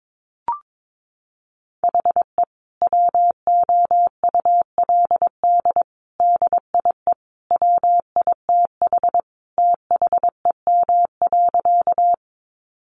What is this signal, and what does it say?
Morse code, heard as a keyed tone, about 22 wpm, 700 Hz. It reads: HE WOULD DIE WITH THEM.
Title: Sentences from Top 1300 Words 22wpm